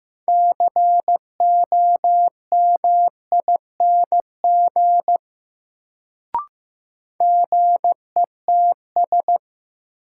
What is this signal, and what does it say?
Morse code, heard as a keyed tone, about 15 wpm, 700 Hz. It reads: COMING GETS
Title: Sentences from Top 1300 Words - New Word Drill - 15wpm